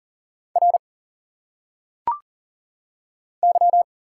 Morse code, heard as a keyed tone, about 39 wpm, 700 Hz. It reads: R Y
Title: Single Letters - Phonetic Pronunciation 40wpm